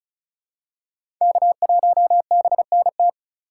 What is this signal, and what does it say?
Morse code, heard as a keyed tone, about 36 wpm, 700 Hz. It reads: K1BNT
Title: USA Call Signs - 5 Characters - 2x Character Spacing - 35wpm